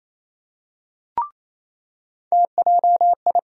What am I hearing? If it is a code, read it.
Morse code, heard as a keyed tone, about 28 wpm, 700 Hz. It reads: TJI
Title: USA Call Sign Postfixes - 3 Letters 28wpm